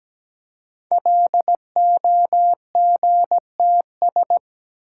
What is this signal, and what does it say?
Morse code, heard as a keyed tone, about 17 wpm, 700 Hz. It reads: LOGTS
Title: Code Groups - 5 Letter:Numbers - 17wpm